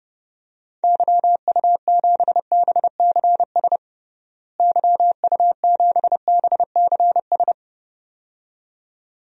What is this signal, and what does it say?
Morse code, heard as a keyed tone, about 30 wpm, 700 Hz. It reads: YU7BCS YU7BCS
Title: DXCC-100 Call Signs - 6 Characters - Repeated 2x - 30wpm